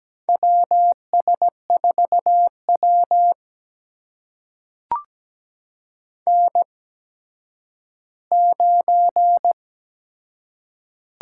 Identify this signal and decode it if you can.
Morse code, heard as a keyed tone, about 17 wpm, spaced as for 10 wpm, 700 Hz. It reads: WS4W N9
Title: USA Call Signs - 4 Characters - 8x Character Spacing - 17wpm